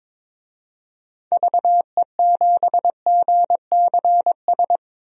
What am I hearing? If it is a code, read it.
Morse code, heard as a keyed tone, about 22 wpm, 700 Hz. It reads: VE7GCS
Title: DXCC-100 Call Signs - 6 Characters - 22wpm